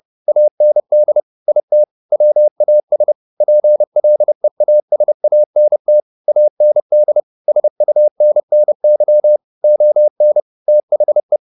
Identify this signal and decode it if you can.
Morse code, heard as a keyed tone, about 30 wpm, 600 Hz. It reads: AND IT WAS PLEASANT AND SUNNY ON THE
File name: the old man and the sea / the old man and the sea 1 30wpm